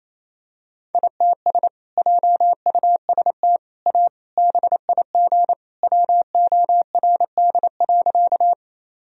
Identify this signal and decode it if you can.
Morse code, heard as a keyed tone, about 28 wpm, 700 Hz. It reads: ITS JUST A BIG WORD.
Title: Sentences from Top 1000 Words - Review Part 1 - 28wpm